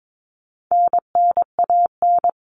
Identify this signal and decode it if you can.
Morse code, heard as a keyed tone, about 22 wpm, 700 Hz. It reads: NNAN
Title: All Cut Numbers - 4-Digit 22wpm